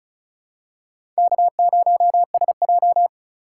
Morse code, heard as a keyed tone, about 34 wpm, 700 Hz. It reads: K0SJ
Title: USA Call Signs - 4 Characters - 2x Character Spacing - 35wpm